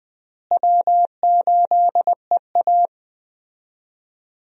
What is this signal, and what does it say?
Morse code, heard as a keyed tone, about 20 wpm, 700 Hz. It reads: W8EA
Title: USA Call Signs - 4 Characters - Repeated 3x - 20wpm